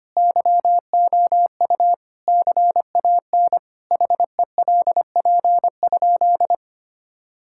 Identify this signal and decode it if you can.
Morse code, heard as a keyed tone, about 25 wpm, 700 Hz. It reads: YOU CAN HELP?
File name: practice / 25wpm / Sentences with 4-Word Phrases - 25wpm